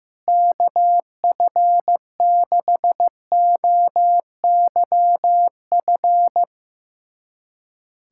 Morse code, heard as a keyed tone, about 16 wpm, 700 Hz. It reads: KF6OYF
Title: USA Callsign DE Callsign - 6 Characters - 15wpm